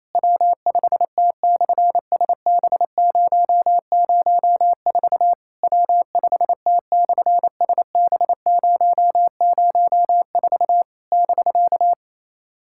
Morse code, heard as a keyed tone, about 27 wpm, 700 Hz. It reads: W5T/SB004 W5T/SB004 <BK>
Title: SOTA QSO Practice 28wpm